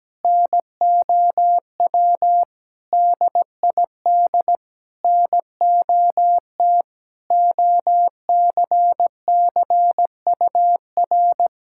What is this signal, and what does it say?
Morse code, heard as a keyed tone, about 17 wpm, 700 Hz. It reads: NOW DID NOT OCCUR
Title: Sentences from Top 1000 Words - Review Part 6 - 17wpm